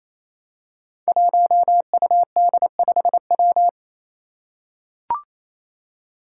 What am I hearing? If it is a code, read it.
Morse code, heard as a keyed tone, about 27 wpm, 700 Hz. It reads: 1UD5W
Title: Five Challenging Letter-Number Combinations 28wpm